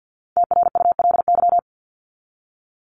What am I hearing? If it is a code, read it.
Morse code, heard as a keyed tone, about 50 wpm, 700 Hz. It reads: TRULY
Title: LY Suffix - Top 100 Words with Base Words 50wpm